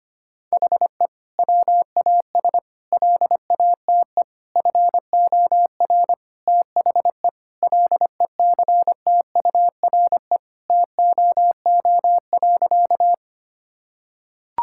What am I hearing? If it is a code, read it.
Morse code, heard as a keyed tone, about 25 wpm, 700 Hz. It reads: HE WAS LATE FOR THE LECTURE TOO.
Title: Sentences from Top 1600 Words 25wpm